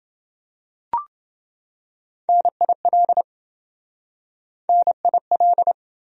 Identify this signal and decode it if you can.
Morse code, heard as a keyed tone, about 26 wpm, 700 Hz. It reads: NIL NIL